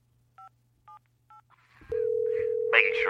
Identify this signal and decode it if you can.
DTMF, heard as telephone touch tones, tones of about 120 ms, with speech. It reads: 578